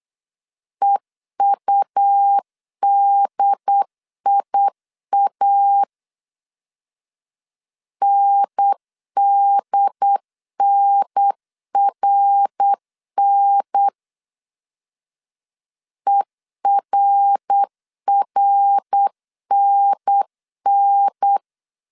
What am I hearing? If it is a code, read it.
Morse code, heard as a keyed tone, about 8 wpm, 800 Hz. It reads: EUDIA NDNRN ERRNN
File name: CW / DET / 1 3 1500C S B 8wpm 4141